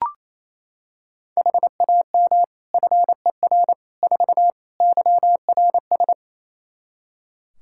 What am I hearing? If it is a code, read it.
Morse code, heard as a keyed tone, about 28 wpm, 700 Hz. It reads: HAM FER 4 YRS
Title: Practice QSO - Exchange 3 - More Info - 28wpm